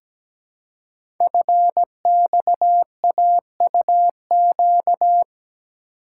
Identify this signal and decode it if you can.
Morse code, heard as a keyed tone, about 17 wpm, 700 Hz. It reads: FXAUQ
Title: Code Groups - 5 Letters - 17wpm